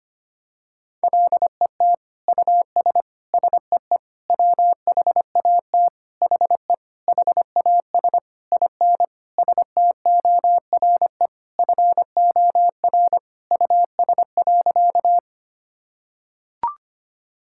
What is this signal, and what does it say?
Morse code, heard as a keyed tone, about 25 wpm, 700 Hz. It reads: LET US SEE WHAT HE HAS IN STORE FOR US.
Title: Sentences from Top 1000 Words - Review Part 5 - 25wpm